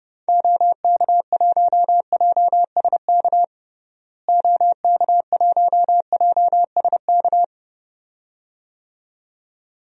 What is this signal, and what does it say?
Morse code, heard as a keyed tone, about 32 wpm, 700 Hz. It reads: OK1JSK OK1JSK
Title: DXCC-100 Call Signs - 6 Characters - Repeated 2x - 30wpm